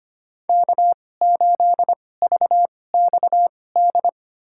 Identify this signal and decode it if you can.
Morse code, heard as a keyed tone, about 25 wpm, spaced as for 18 wpm, 700 Hz. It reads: K8VXD
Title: USA Call Signs - 5 Characters - 2x Character Spacing - 25wpm